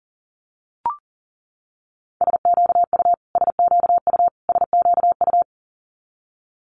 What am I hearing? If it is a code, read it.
Morse code, heard as a keyed tone, about 40 wpm, 700 Hz. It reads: SQU SQU SQU
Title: Multi-letter Phonemes Repeated 3x 40wpm